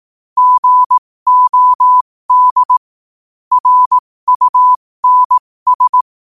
Morse code, heard as a keyed tone, about 18 wpm, spaced as for 15 wpm, 1000 Hz. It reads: GOD RUNS
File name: qotdfiles / 15wpm